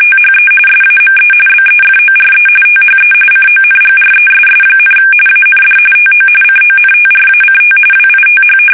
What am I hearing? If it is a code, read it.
RTTY, about 45.45 baud. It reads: THE SAME BEFORE THE CONGRESS, AND TO CONTRACT WITH
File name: fsk850 / usnavy1